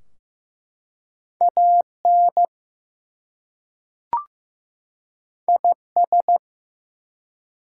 Morse code, heard as a keyed tone, about 15 wpm, 700 Hz. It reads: AN IS
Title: Two Letter Words - 15wpm